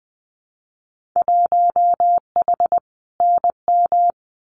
Morse code, heard as a keyed tone, about 20 wpm, 700 Hz. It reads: 1H NM
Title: Winter Field Day Exchanges 20wpm